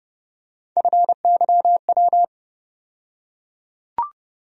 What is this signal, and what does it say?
Morse code, heard as a keyed tone, about 30 wpm, 700 Hz. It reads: FYW
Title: USA Call Sign Postfixes - 3 Letters 30wpm